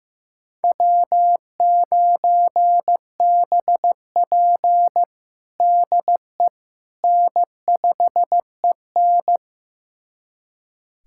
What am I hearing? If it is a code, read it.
Morse code, heard as a keyed tone, about 15 wpm, 700 Hz. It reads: W9BP DE N5EN
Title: USA Callsign DE Callsign - 4 Characters - 15wpm